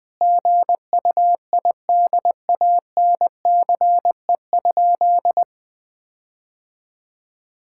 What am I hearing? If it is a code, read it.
Morse code, heard as a keyed tone, about 20 wpm, 700 Hz. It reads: GUIDANCE?
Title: Sentences from Top 1600 Words 20wpm